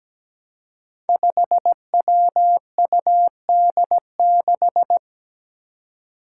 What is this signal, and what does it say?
Morse code, heard as a keyed tone, about 17 wpm, 700 Hz. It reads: 5WUD6
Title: Five Challenging Letter-Number Combinations 17wpm